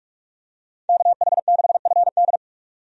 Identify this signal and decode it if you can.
Morse code, heard as a keyed tone, about 42 wpm, 700 Hz. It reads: KH6FD
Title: USA Call Signs - 5 Characters - 45wpm